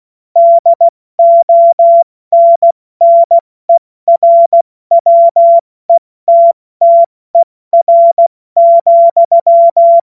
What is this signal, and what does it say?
Morse code, heard as a keyed tone, about 16 wpm, 680 Hz. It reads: DONNERWETTER,
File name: Morsekurs / Witze / Witz 24 Juni Mai 2025 70 F80